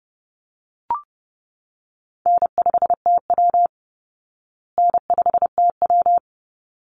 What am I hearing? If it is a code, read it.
Morse code, heard as a keyed tone, about 30 wpm, 700 Hz. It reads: N5TW N5TW
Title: Easy Call Signs Repeated 2x - 30wpm